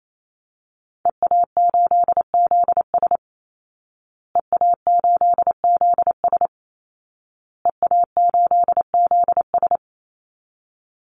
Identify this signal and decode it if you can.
Morse code, heard as a keyed tone, about 28 wpm, 700 Hz. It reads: EA8ZS EA8ZS EA8ZS
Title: Call Signs Repeated 2x - 28wpm